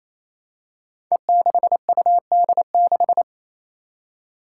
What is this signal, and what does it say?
Morse code, heard as a keyed tone, about 28 wpm, 700 Hz. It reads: E6UD6
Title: Five Challenging Letter-Number Combinations 28wpm